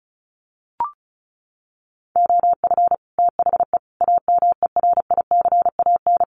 Morse code, heard as a keyed tone, about 34 wpm, 700 Hz. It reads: OF THE AMERICAN
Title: Sets of 3 Words 35wpm